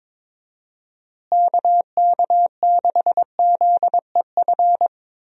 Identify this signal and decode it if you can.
Morse code, heard as a keyed tone, about 21 wpm, 700 Hz. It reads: KK6ZEF
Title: USA Call Signs - 6 Characters - Repeated 3x - 22wpm